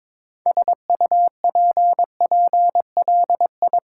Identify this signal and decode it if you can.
Morse code, heard as a keyed tone, about 23 wpm, 700 Hz. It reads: SUPPLI
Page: morse-code-ninja-practice-22wpm